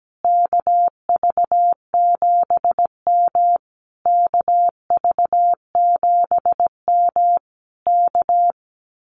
Morse code, heard as a keyed tone, about 17 wpm, 700 Hz. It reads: KV7M KV7M K
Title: Common QSO Phrases Encore 17wpm